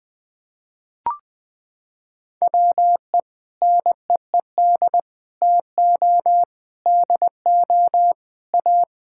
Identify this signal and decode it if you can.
Morse code, heard as a keyed tone, about 21 wpm, 700 Hz. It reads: WE NEED TO DO A
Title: Sentences with 7-Word Phrases - 20wpm